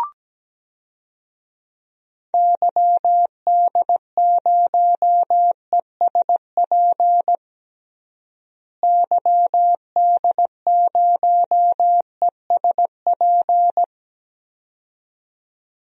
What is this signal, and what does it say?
Morse code, heard as a keyed tone, about 17 wpm, 700 Hz. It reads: YD0ESP YD0ESP
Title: DXCC-100 Call Signs - 6 Characters - Repeated 2x - 17wpm